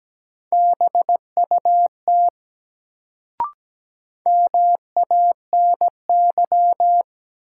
Morse code, heard as a keyed tone, about 17 wpm, 700 Hz. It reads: BUT MANY
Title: Sentences from Top 100 Words - New Word Drill - 17wpm